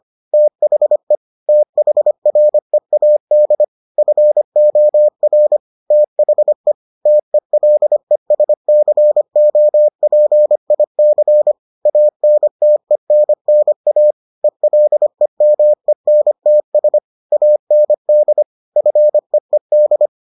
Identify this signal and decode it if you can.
Morse code, heard as a keyed tone, about 25 wpm, 600 Hz. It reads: THE THREAD FOR THE TELESCOPIC ANTENNA ELEMENTS AND FEED